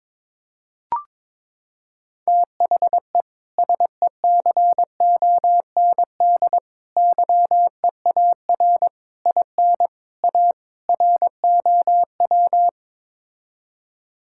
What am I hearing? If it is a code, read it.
Morse code, heard as a keyed tone, about 21 wpm, 700 Hz. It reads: THE SECOND YEAR IN A ROW
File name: practice / 22wpm / Sets of 6 Words - 22wpm